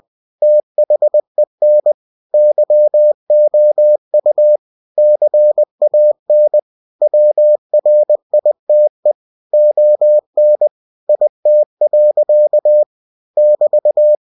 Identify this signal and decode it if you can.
Morse code, heard as a keyed tone, about 20 wpm, 600 Hz. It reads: THEN YOU CAN WRITE ON IT. <BT>
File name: cw-books / winnie-the-pooh / winnie-20wpm-page-0051